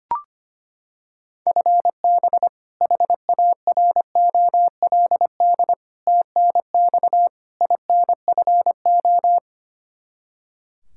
Morse code, heard as a keyed tone, about 25 wpm, 700 Hz. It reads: FB HAROLD TNX INFO
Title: Practice QSO - Exchange 3 - More Info - 25wpm